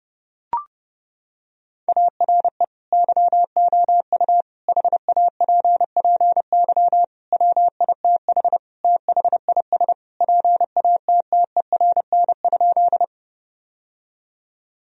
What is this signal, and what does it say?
Morse code, heard as a keyed tone, about 32 wpm, 700 Hz. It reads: ARE YOU HAPPY WITH THIS PATTERN?
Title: Sentences from Top 1000 Words - Review Part 2 - 30wpm